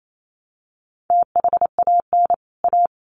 Morse code, heard as a keyed tone, about 28 wpm, 700 Hz. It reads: THAN A
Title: Sets of 2 Words 28wpm